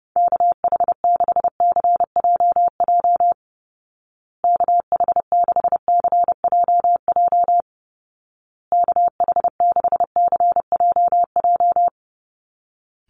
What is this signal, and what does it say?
Morse code, heard as a keyed tone, about 30 wpm, 700 Hz. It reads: KH6CJJ KH6CJJ KH6CJJ
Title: Call Signs Repeated 2x - 30wpm